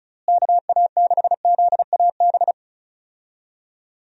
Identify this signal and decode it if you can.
Morse code, heard as a keyed tone, about 36 wpm, 700 Hz. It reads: KA6ZAB
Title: USA Call Signs - 6 Characters - Repeated 2x - 35wpm